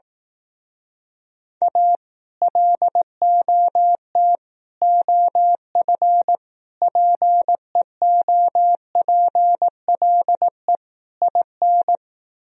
Morse code, heard as a keyed tone, about 18 wpm, 700 Hz. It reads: A LOT OF PEOPLE IN
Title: Speed-Racing - Sets of 5 Words 15wpm